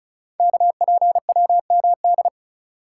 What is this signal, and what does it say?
Morse code, heard as a keyed tone, about 36 wpm, 700 Hz. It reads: KPWMD
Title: Code Groups - 5 Letters - 35wpm